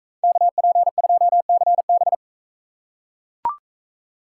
Morse code, heard as a keyed tone, about 42 wpm, 700 Hz. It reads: KP2CB